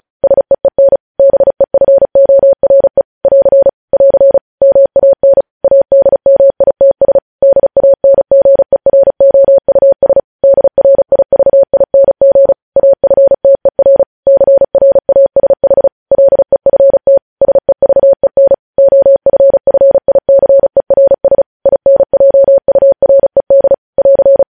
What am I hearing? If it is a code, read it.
Morse code, heard as a keyed tone, about 36 wpm, 550 Hz. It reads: SEEN BEFORE <AR> <AR> MAN ADMITS DANGEROUS DRIVING AFTER CRASH LEFT SEVEN OFFICERS INJURED <AR>
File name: podcast / News headlines-1x-35x35